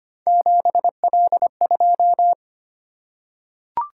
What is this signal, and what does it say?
Morse code, heard as a keyed tone, about 25 wpm, 700 Hz. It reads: 7L2
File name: Three Letter-Number Combinations - 25wpm